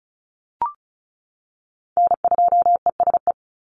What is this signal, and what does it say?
Morse code, heard as a keyed tone, about 34 wpm, 700 Hz. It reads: N2ESE
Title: NA SOTA Chaser Callsigns 35wpm